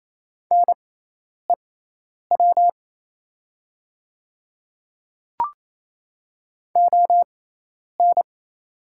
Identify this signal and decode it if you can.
Morse code, heard as a keyed tone, about 28 wpm, spaced as for 10 wpm, 700 Hz. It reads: NEW ON
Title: 6x Character Spacing - Top 200 Words - 28wpm